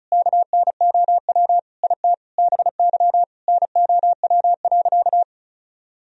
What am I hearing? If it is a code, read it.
Morse code, heard as a keyed tone, about 34 wpm, 700 Hz. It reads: KNOW IT BY NOW.
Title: Sentences from Top 1000 Words - Review Part 2 - 35wpm